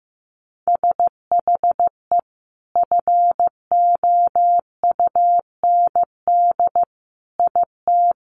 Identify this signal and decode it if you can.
Morse code, heard as a keyed tone, about 15 wpm, 700 Hz. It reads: SHE FOUND IT
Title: Sentences from Top 1100 Words 15wpm